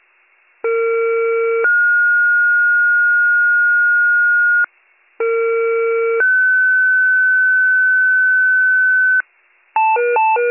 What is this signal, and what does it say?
Two-tone page, heard as a one-second tone, then a three-second tone, about 460 then 1430 Hz, about 460 then 1500 Hz. It is heard from 0.6 to 9.2 s.